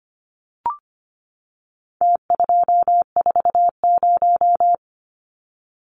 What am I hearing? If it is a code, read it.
Morse code, heard as a keyed tone, about 25 wpm, 700 Hz. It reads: T240
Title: Numbers - Contest Style - 4-Digit 25wpm